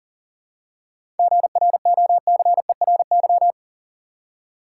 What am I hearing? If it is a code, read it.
Morse code, heard as a keyed tone, about 40 wpm, 700 Hz. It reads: GROCERY
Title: Seven Letter Words - 40wpm